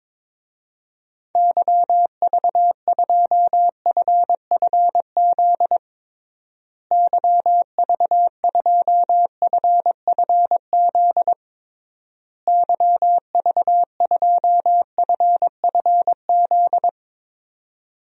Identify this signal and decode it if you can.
Morse code, heard as a keyed tone, about 22 wpm, 700 Hz. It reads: YV2FFZ YV2FFZ YV2FFZ
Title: DXCC-100 Call Signs - 6 Characters - Repeated 3x - 22wpm